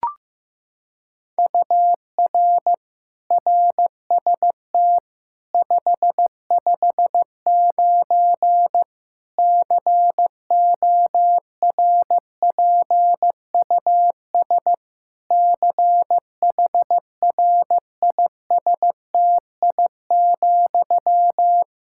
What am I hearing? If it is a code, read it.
Morse code, heard as a keyed tone, about 16 wpm, 700 Hz. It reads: UR RST 559 CORPUS CHRISTI,
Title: Practice QSO - Exchange 1 - Beginning - 15wpm